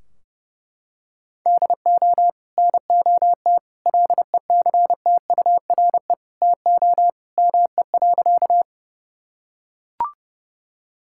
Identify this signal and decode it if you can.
Morse code, heard as a keyed tone, about 30 wpm, 700 Hz. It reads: DO NOT LECTURE TO ME.
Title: Sentences from Top 1600 Words 30wpm